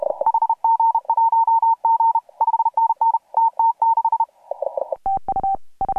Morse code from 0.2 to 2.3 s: HG1G